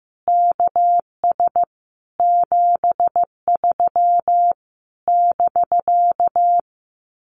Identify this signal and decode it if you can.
Morse code, heard as a keyed tone, about 16 wpm, 700 Hz. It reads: KS 73 <BK>
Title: Common QSO Phrases Encore 15wpm